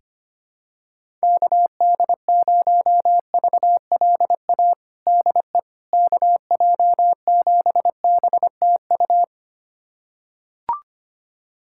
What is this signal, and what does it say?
Morse code, heard as a keyed tone, about 25 wpm, 700 Hz. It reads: KD0VLA DE KJ7BTU
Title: USA Callsign DE Callsign - 6 Characters - 25wpm